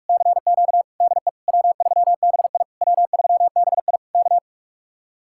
Morse code, heard as a keyed tone, about 59 wpm, spaced as for 52 wpm, 700 Hz. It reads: CQ DE W3BI W3BI K
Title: Speed-Racing Common QSO Phrases - 45wpm